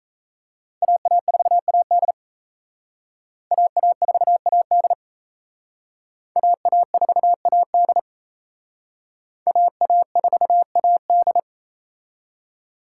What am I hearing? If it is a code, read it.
Morse code, heard as a keyed tone, about 34 wpm, 700 Hz. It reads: AA4AD AA4AD AA4AD AA4AD
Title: Speed Racing - USA 2x2 Call Signs 28wpm